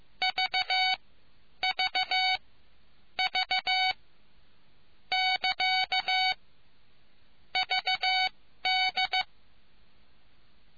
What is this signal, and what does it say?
Morse code, heard as a keyed tone, about 15 wpm, spaced as for 12 wpm, 750 Hz. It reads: V V V <KA> VD